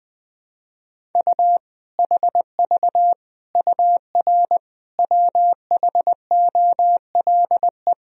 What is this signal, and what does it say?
Morse code, heard as a keyed tone, about 21 wpm, 700 Hz. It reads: U HV UR WHOLE
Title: Sentences from Top 400 Words and Abbreviations 20wpm